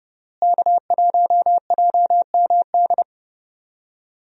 Morse code, heard as a keyed tone, about 32 wpm, 700 Hz. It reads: K1JMD
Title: USA Call Signs - 5 Characters - Repeated 2x - 30wpm